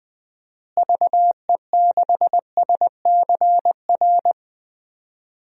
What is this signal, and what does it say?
Morse code, heard as a keyed tone, about 21 wpm, 700 Hz. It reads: VE6SCR
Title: DXCC-100 Call Signs - 6 Characters - Repeated 3x - 20wpm